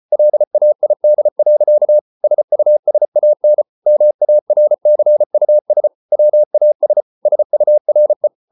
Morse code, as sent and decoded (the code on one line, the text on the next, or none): .-.. .- .. -.. .-.-.- / ... ..- ... .- -. / -- .- .-. -.-. ..- ... / .-- .- ... / ... ..- .-. .
LAID. SUSAN MARCUS WAS SURE